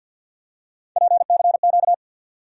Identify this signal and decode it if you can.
Morse code, heard as a keyed tone, about 53 wpm, 700 Hz. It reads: PCQ
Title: USA Call Sign Postfixes - 3 Letters 50wpm